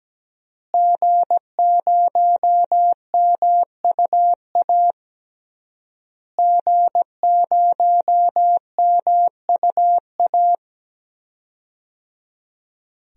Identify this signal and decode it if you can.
Morse code, heard as a keyed tone, about 17 wpm, 700 Hz. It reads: G0MUA G0MUA
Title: DXCC-100 Call Signs - 5 Characters - Repeated 3x - 17wpm